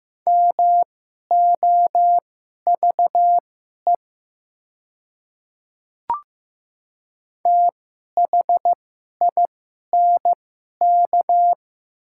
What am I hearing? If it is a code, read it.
Morse code, heard as a keyed tone, about 15 wpm, spaced as for 11 wpm, 700 Hz. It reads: MOVE THINK
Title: 2x Character Spacing - Top 200 Words - 15wpm